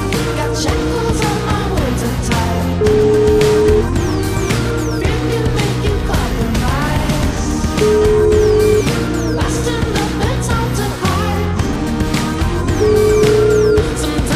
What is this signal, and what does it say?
Ringback tone (Europe, 425 Hz), with noise.